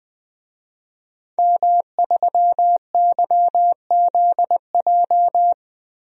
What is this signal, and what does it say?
Morse code, heard as a keyed tone, about 20 wpm, 700 Hz. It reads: M3YZJ
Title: DXCC-100 Call Signs - 5 Characters - 20wpm